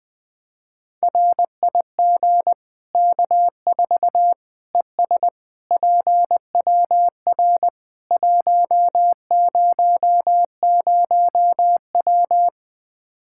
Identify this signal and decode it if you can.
Morse code, heard as a keyed tone, about 20 wpm, 700 Hz. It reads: RIG K4 ES PWR 100W
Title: Practice QSO - Exchange 2 - Info - 20wpm